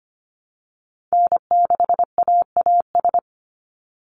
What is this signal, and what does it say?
Morse code, heard as a keyed tone, about 25 wpm, 700 Hz. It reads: N6AAS
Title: USA Call Signs 25wpm